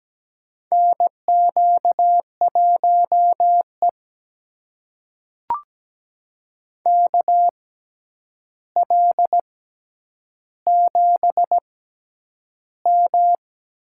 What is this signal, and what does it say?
Morse code, heard as a keyed tone, about 17 wpm, spaced as for 9 wpm, 700 Hz. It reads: NQ1E KL7M
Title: USA Call Signs - 6x Character Spacing - 17wpm